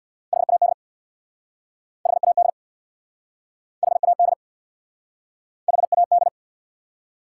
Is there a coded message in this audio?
Yes: Morse code from 0.3 to 6.3 s, heard as a keyed tone, about 69 wpm, 700 Hz. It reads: HAD HAD HAD HAD